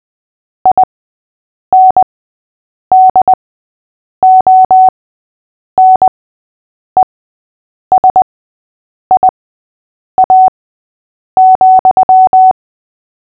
Morse code, heard as a keyed tone, about 20 wpm, spaced as for 8 wpm, 750 Hz. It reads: INDONESIA,
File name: cw / fists / Short-Story-3-8wpm